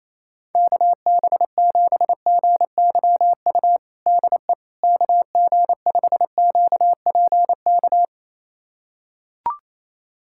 Morse code, heard as a keyed tone, about 28 wpm, 700 Hz. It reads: KB7GYU DE KG5QPK
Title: USA Callsign DE Callsign - 6 Characters - 28wpm